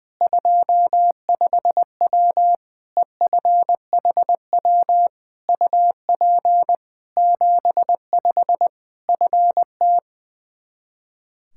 Morse code, heard as a keyed tone, about 20 wpm, 700 Hz. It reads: 25W EFHW UP 75 FT
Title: K1USN SST QSO Practice 20wpm